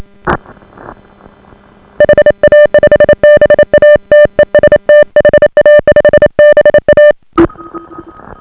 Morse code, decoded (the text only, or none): HA5BA TEST HA5BA